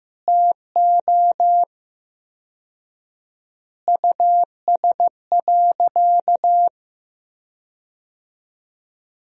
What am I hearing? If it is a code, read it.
Morse code, heard as a keyed tone, about 15 wpm, 700 Hz. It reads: TO US.
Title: 4x Word-Spacing - Sentences from Top 500 Words - 15wpm